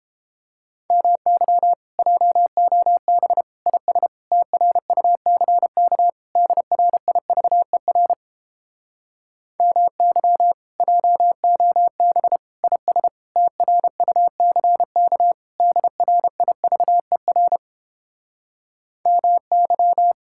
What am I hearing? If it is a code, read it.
Morse code, heard as a keyed tone, about 31 wpm, 700 Hz. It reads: MY JOB IS TRUCK DRIVER MY JOB IS TRUCK DRIVER MY